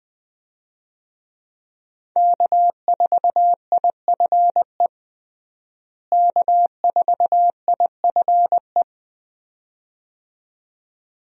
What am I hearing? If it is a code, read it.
Morse code, heard as a keyed tone, about 20 wpm, 700 Hz. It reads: K4IFE K4IFE
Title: USA Call Signs - 5 Characters - Repeated 2x - 20wpm